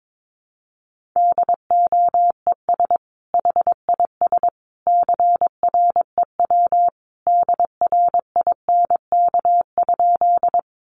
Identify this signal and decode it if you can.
Morse code, heard as a keyed tone, about 22 wpm, 700 Hz. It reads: DOES HIS CREW DRINK?
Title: Sentences from Top 1500 Words 22wpm